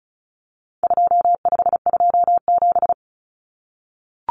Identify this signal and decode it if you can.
Morse code, heard as a keyed tone, about 34 wpm, 700 Hz. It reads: 2527
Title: Numbers - 4-Digit 35wpm